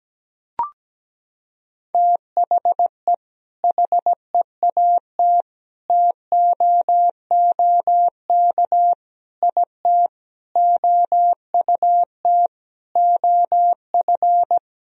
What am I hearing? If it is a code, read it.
Morse code, heard as a keyed tone, about 17 wpm, 700 Hz. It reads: THE HEAT TOOK IT OUT OF
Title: Sentences from Top 1000 Words - Review Part 3 - 17wpm